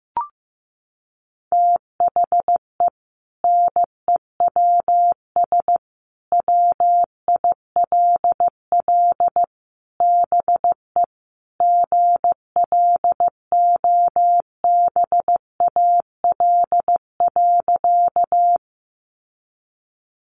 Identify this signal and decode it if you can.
Morse code, heard as a keyed tone, about 15 wpm, 700 Hz. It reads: THE NEWS WILL BE GLOBAL.
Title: Sentences from Top 1200 Words 15wpm